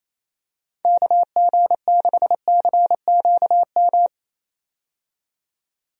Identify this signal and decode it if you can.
Morse code, heard as a keyed tone, about 28 wpm, 700 Hz. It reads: KG6CQM
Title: USA Call Signs - 6 Characters - 28wpm